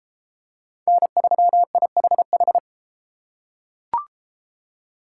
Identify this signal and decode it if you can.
Morse code, heard as a keyed tone, about 33 wpm, 700 Hz. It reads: N3IHH